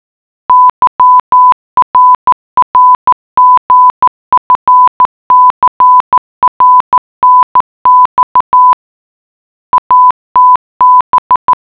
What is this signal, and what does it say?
Morse code, heard as a keyed tone, about 14 wpm, 1000 Hz. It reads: YRRGFCRNX ATB